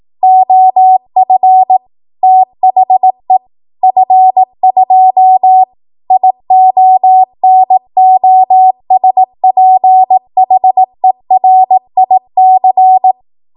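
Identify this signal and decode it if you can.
Morse code, heard as a keyed tone, about 18 wpm, 750 Hz. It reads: OF THE F2 IONOSPHERIC
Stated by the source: 18 WPM Morse code practice from a national amateur radio association's code archive.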